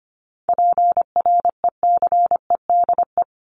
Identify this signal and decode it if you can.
Morse code, heard as a keyed tone, about 25 wpm, 700 Hz. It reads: PRECEDE
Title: ING Suffix - Top 100 Words with Base Words 25wpm